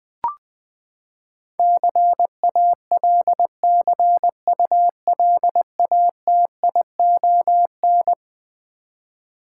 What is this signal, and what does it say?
Morse code, heard as a keyed tone, about 20 wpm, 700 Hz. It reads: CALCULATION
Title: Eleven Letter Words - 20wpm